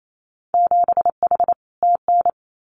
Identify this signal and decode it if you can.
Morse code, heard as a keyed tone, about 28 wpm, 700 Hz. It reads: 7H TN
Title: Winter Field Day Exchanges Encore 28wpm